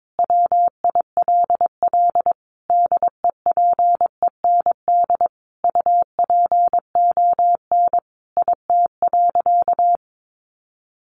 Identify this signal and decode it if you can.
Morse code, heard as a keyed tone, about 22 wpm, 700 Hz. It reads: WILL DEPEND UPON IT.
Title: Sentences from Top 1300 Words 22wpm